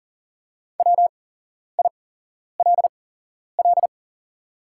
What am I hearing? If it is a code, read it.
Morse code, heard as a keyed tone, about 38 wpm, spaced as for 11 wpm, 700 Hz. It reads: WILL
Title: 8x Character Spacing - Top 300 Words - 40wpm